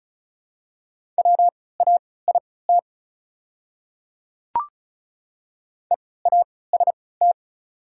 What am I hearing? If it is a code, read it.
Morse code, heard as a keyed tone, about 34 wpm, spaced as for 20 wpm, 690 Hz. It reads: WAIT EAST
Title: 3x Character Spacing - Top 500 Words - 35wpm